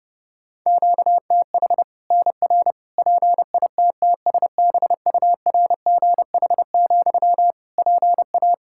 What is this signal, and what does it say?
Morse code, heard as a keyed tone, about 30 wpm, 700 Hz. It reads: QTH NR PITTSBURGH, PA
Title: Practice QSO - Exchange 1 - Beginning - 30wpm